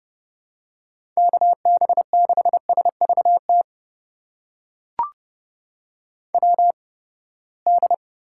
Morse code, heard as a keyed tone, about 31 wpm, spaced as for 21 wpm, 700 Hz. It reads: KB6SVT WD
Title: USA Call Signs - 6 Characters - 8x Character Spacing - 30wpm